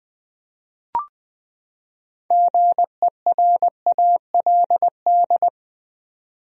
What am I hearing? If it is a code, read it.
Morse code, heard as a keyed tone, about 20 wpm, 700 Hz. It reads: GERALD
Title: USA Common Names - 20wpm